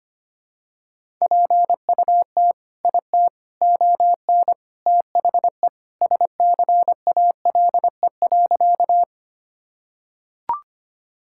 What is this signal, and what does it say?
Morse code, heard as a keyed tone, about 25 wpm, 700 Hz. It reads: PUT IT ON THE SCALE.
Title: Sentences from Top 1000 Words - Review Part 5 - 25wpm